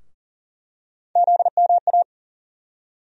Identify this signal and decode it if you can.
Morse code, heard as a keyed tone, about 40 wpm, 700 Hz. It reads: ZMA
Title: USA Call Sign Postfixes - 3 Letters 40wpm